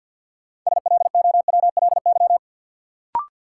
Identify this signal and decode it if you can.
Morse code, heard as a keyed tone, about 47 wpm, 700 Hz. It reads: SLOWLY
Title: Sentences from Top 1100 Words - New Word Drill Alt - 50wpm